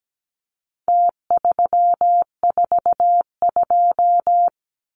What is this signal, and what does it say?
Morse code, heard as a keyed tone, about 17 wpm, 700 Hz. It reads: T342
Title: Numbers - Contest Style - 4-Digit 17wpm